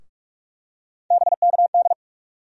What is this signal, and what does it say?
Morse code, heard as a keyed tone, about 42 wpm, 700 Hz. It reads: BKD
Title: USA Call Sign Postfixes - 3 Letters 45wpm